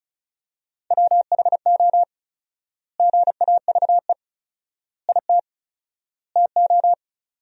Morse code, heard as a keyed tone, about 36 wpm, 700 Hz. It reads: WHO GAVE IT TO
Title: 4x Word-Spacing - Sentences from Top 500 Words - 35wpm